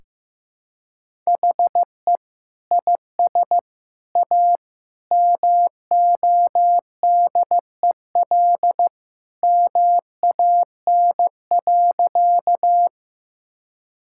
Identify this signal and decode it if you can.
Morse code, heard as a keyed tone, about 15 wpm, 700 Hz. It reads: HE IS A MODEL MAN.
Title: Sentences from Top 1000 Words - Review Part 5 - 15wpm